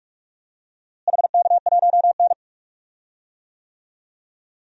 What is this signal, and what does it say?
Morse code, heard as a keyed tone, about 45 wpm, 700 Hz. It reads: HK1N
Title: DXCC-100 Call Signs - 4 Characters - Repeated 3x - 45wpm